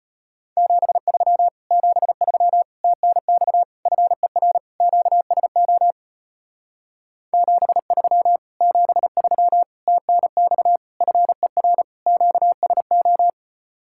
Morse code, heard as a keyed tone, about 33 wpm, 700 Hz. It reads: 73 73 TNX FER QSO 73 73 TNX FER QSO